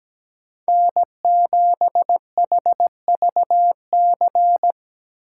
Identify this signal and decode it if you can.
Morse code, heard as a keyed tone, about 17 wpm, 700 Hz. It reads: N7HVC
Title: USA Call Signs - 5 Characters - 3x Character Spacing - 17wpm